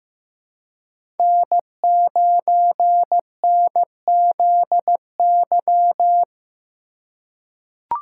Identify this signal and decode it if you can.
Morse code, heard as a keyed tone, about 15 wpm, 700 Hz. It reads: N9NZY
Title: USA Call Signs - 5 Characters - 3x Character Spacing - 15wpm